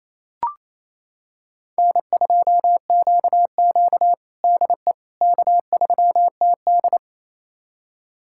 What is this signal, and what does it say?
Morse code, heard as a keyed tone, about 28 wpm, 700 Hz. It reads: N2QQ DE K3TD
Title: USA Callsign DE Callsign - 4 Characters - 28wpm